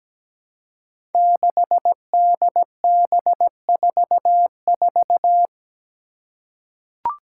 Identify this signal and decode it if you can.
Morse code, heard as a keyed tone, about 17 wpm, 700 Hz. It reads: 6DB44
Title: Five Challenging Letter-Number Combinations 17wpm